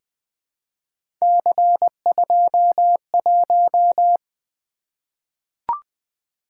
Morse code, heard as a keyed tone, about 20 wpm, 700 Hz. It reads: C21
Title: Three Letter-Number Combinations - 20wpm